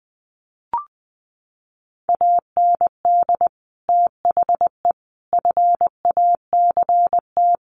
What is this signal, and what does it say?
Morse code, heard as a keyed tone, about 19 wpm, 700 Hz. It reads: AND THE FACT
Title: Sets of 4 Words 20wpm